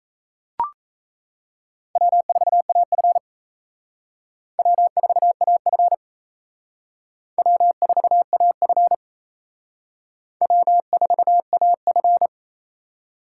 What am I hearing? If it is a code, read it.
Morse code, heard as a keyed tone, about 34 wpm, 700 Hz. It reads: W4AF W4AF W4AF W4AF
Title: Speed Racing - USA 1x2 Call Signs 28wpm